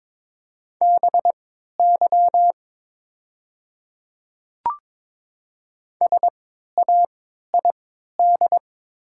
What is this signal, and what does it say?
Morse code, heard as a keyed tone, about 22 wpm, spaced as for 12 wpm, 700 Hz. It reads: BY SAID
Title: 3x Character Spacing - Top 100 Words - 22wpm